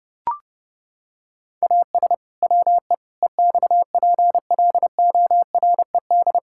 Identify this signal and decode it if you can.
Morse code, heard as a keyed tone, about 32 wpm, 700 Hz. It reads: AS WE EXPLORED
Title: Pangram Challenge v2 - 30wpm